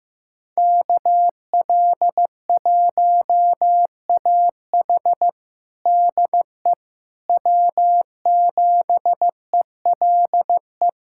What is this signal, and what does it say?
Morse code, heard as a keyed tone, about 15 wpm, 700 Hz. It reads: KL1AH DE W7ELE
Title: USA Callsign DE Callsign 15wpm